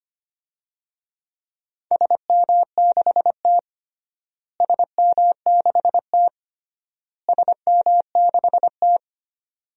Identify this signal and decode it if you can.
Morse code, heard as a keyed tone, about 25 wpm, 700 Hz. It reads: SM6T SM6T SM6T
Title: DXCC-100 Call Signs - 4 Characters - Repeated 3x - 25wpm